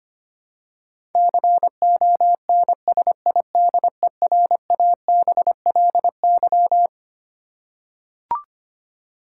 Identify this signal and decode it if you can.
Morse code, heard as a keyed tone, about 25 wpm, 700 Hz. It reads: CONSIDERABLY
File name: practice / 25wpm / Twelve Letter Words - 25wpm